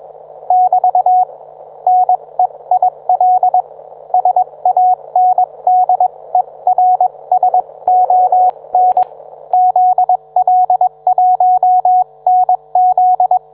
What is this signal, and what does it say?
Morse code, heard as a keyed tone, about 21 wpm, spaced as for 19 wpm, 730 Hz. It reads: <BT> NEIL SANDERSON ZL1NZ
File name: book